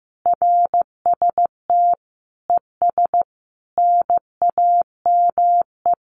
Morse code, heard as a keyed tone, about 15 wpm, 700 Hz. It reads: RST ES NAME
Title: Common QSO Phrases 15wpm